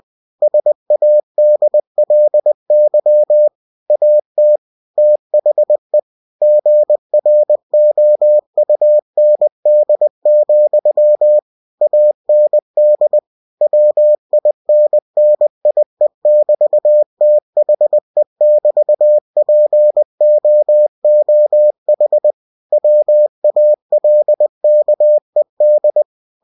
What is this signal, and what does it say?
Morse code, heard as a keyed tone, about 20 wpm, 600 Hz. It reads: SADLY AT THE GROUND, AND WINNIE<BT>THE<BT>POOH WALKED
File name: cw-books / winnie-the-pooh / winnie-20wpm-page-0026